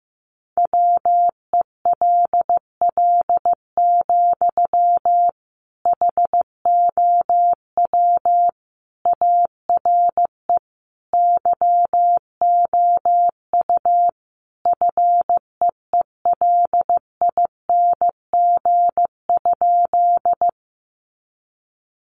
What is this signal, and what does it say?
Morse code, heard as a keyed tone, about 15 wpm, 700 Hz. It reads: WELL, HOW ARE YOU FEELING?
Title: Sentences from Top 1200 Words 15wpm